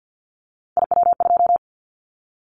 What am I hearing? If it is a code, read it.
Morse code, heard as a keyed tone, about 50 wpm, 700 Hz. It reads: IW2
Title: Callsign Prefixes for Top 25 DXCC - 50wpm